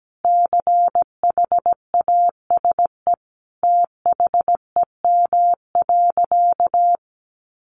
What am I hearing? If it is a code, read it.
Morse code, heard as a keyed tone, about 17 wpm, 700 Hz. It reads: CHASE THEM.
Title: Additional Easy Phrases 17wpm